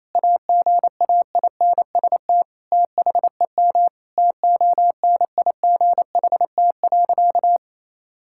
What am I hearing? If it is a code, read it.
Morse code, heard as a keyed tone, about 28 wpm, 700 Hz. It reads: AGAINST THEM TONIGHT.
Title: Sentences from Top 1000 Words - Review Part 6 - 28wpm